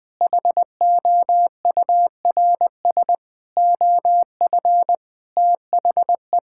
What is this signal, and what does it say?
Morse code, heard as a keyed tone, about 20 wpm, 700 Hz. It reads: HOURS OF THE
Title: Sentences with 7-Word Phrases - 20wpm